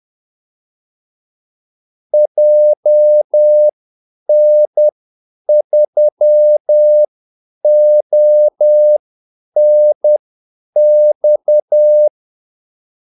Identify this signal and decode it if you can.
Morse code, heard as a keyed tone, about 10 wpm, spaced as for 8 wpm, 600 Hz. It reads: JN3ONX
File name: Callsigns-10-8wpm